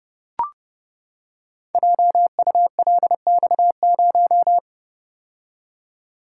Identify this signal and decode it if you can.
Morse code, heard as a keyed tone, about 29 wpm, 700 Hz. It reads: JULX0
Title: Code Groups - 5 Letter:Numbers - 30wpm